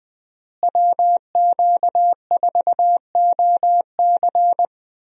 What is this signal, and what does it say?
Morse code, heard as a keyed tone, about 20 wpm, 700 Hz. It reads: WQ4OC
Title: USA Call Signs - 5 Characters - Repeated 3x - 20wpm